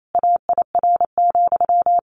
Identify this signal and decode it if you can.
Morse code, heard as a keyed tone, about 28 wpm, 700 Hz. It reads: AIR,
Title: A Sound Of Thunder 28wpm